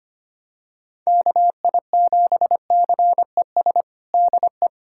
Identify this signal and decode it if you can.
Morse code, heard as a keyed tone, about 25 wpm, 700 Hz. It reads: KI7CES DE
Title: USA Callsign DE Callsign 25wpm